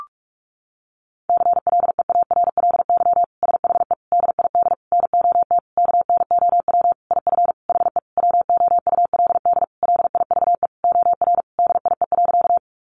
Morse code, heard as a keyed tone, about 44 wpm, 700 Hz. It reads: CLEARLY SHE DID NOT KNOW IF HE WOULD LIVE OR DIE.
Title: Sentences from Top 1300 Words 45wpm